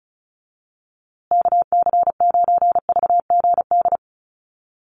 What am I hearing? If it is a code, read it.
Morse code, heard as a keyed tone, about 34 wpm, 700 Hz. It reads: KC9VGD
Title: USA Call Signs Repeated 3x 35wpm